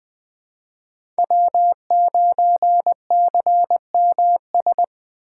Morse code, heard as a keyed tone, about 20 wpm, 700 Hz. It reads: W9CMS
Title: USA Call Signs - 5 Characters - 3x Character Spacing - 20wpm